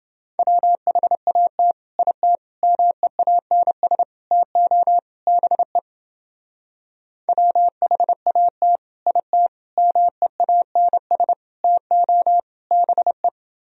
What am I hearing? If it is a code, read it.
Morse code, heard as a keyed tone, about 30 wpm, 700 Hz. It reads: WHAT IT MEANS TO BE WHAT IT MEANS TO BE